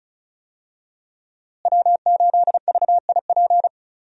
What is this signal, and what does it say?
Morse code, heard as a keyed tone, about 34 wpm, 700 Hz. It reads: W8VIP
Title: USA Call Signs - 5 Characters - Repeated 2x - 35wpm